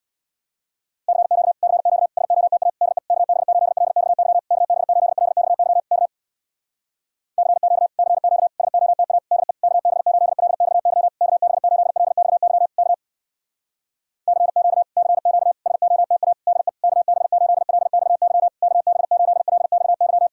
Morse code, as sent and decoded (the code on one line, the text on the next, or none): -.-. --.- / -.-. --.- / ... --- - .- / -.. / -.- -... ---.. .-.. -..- -.-- / -.- -... ---.. .-.. -..- -.-- / -.- / -.-. --.- / -.-. --.- / ... --- - .- / -.. / -.- -... ---.. .-.. -..- -.-- / -.- -... ---.. .-.. -..- -.-- / -.- / -.-. --.- / -.-. --.- / ... --- - .- / -.. / -.- -... ---.. .-.. -..- -.-- / -.- -... ---.. .-.. -..- -.--
CQ CQ SOTA DE KB8LXY KB8LXY K CQ CQ SOTA DE KB8LXY KB8LXY K CQ CQ SOTA DE KB8LXY KB8LXY